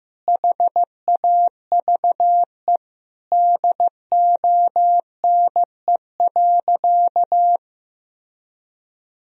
Morse code, heard as a keyed tone, about 16 wpm, 700 Hz. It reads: HAVE DONE.
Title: Sentences from Top 1000 Words - Review Part 3 - 15wpm